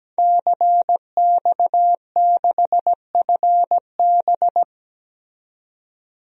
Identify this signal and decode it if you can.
Morse code, heard as a keyed tone, about 17 wpm, 700 Hz. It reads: CX6FB
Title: DXCC-100 Call Signs - 5 Characters - 17wpm